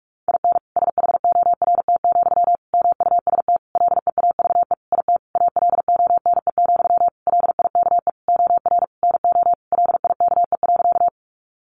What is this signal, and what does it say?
Morse code, heard as a keyed tone, about 44 wpm, 700 Hz. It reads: IN SHORT, MUST LEAVE IT ALONE, LIKE OR NO LIKE.
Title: Sentences from Top 1000 Words 45wpm